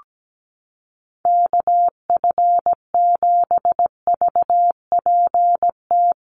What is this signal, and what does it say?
Morse code, heard as a keyed tone, about 17 wpm, 700 Hz. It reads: KF7VPT
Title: USA Call Signs Repeated 3x 17wpm